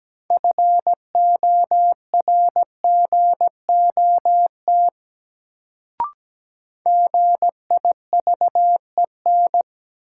Morse code, heard as a keyed tone, about 17 wpm, 700 Hz. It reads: FORGOT GIVEN
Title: Sentences from Top 1400 Words - New Word Drill Alt - 17wpm